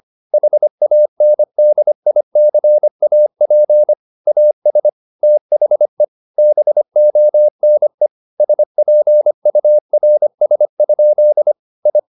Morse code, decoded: HANDICAP AS THE BONE SPURS? I

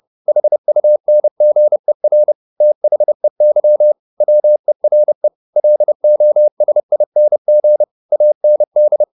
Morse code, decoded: HUNGER THEY WERE LOSING AND

